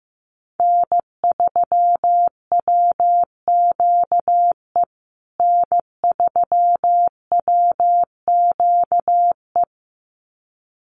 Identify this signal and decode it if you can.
Morse code, heard as a keyed tone, about 15 wpm, 700 Hz. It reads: N3WQE N3WQE
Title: USA Call Signs Repeated 2x 15wpm